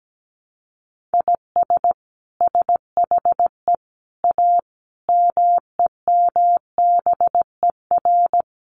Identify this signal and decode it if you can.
Morse code, heard as a keyed tone, about 17 wpm, 700 Hz. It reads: IS SHE A MEMBER
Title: Sentences from Top 1100 Words 17wpm